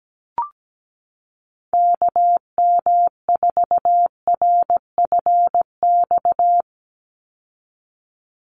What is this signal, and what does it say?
Morse code, heard as a keyed tone, about 17 wpm, 700 Hz. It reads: KM4RFX
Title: USA Call Signs 17wpm